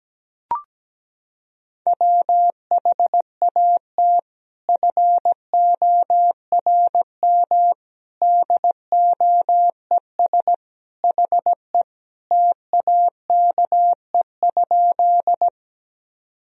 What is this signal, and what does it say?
Morse code, heard as a keyed tone, about 17 wpm, 700 Hz. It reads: WHAT FORM DOES HE TAKE?
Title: Sentences from Top 1000 Words - Review Part 1 - 17wpm